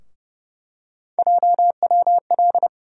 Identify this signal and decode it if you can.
Morse code, heard as a keyed tone, about 30 wpm, 700 Hz. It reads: JWL
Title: USA Call Sign Postfixes - 3 Letters 30wpm